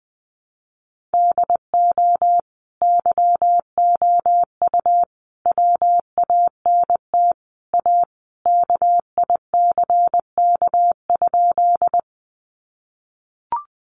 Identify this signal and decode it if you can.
Morse code, heard as a keyed tone, about 20 wpm, 700 Hz. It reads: DO YOU WANT A KICK?
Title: Sentences from Top 1300 Words 20wpm